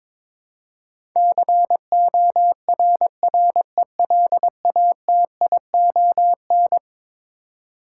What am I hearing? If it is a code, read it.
Morse code, heard as a keyed tone, about 22 wpm, 700 Hz. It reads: CORRELATION
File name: practice / 22wpm / Eleven Letter Words - 22wpm